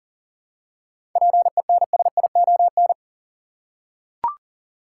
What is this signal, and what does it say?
Morse code, heard as a keyed tone, about 40 wpm, 700 Hz. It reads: PENSION
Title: Seven Letter Words - 40wpm